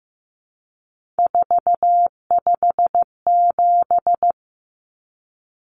Morse code, heard as a keyed tone, about 15 wpm, 700 Hz. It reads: 457
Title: Numbers - 3-Digit 15wpm